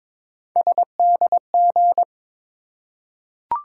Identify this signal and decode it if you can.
Morse code, heard as a keyed tone, about 22 wpm, 700 Hz. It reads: SDG
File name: ARRL Sections v2 - 22wpm